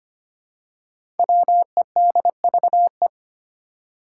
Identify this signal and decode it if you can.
Morse code, heard as a keyed tone, about 25 wpm, 700 Hz. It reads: WEDVE
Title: Five Challenging Letter-Number Combinations 25wpm